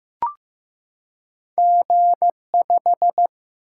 Morse code, heard as a keyed tone, about 15 wpm, 700 Hz. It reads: G5
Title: Single-Letter Single-Number 15wpm